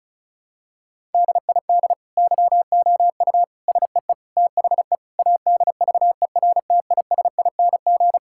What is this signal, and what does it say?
Morse code, heard as a keyed tone, about 36 wpm, 700 Hz. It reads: DID YOU SEE THE ADVERTISING
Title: Sentences from Top 1000 Words - Review Part 6 - 35wpm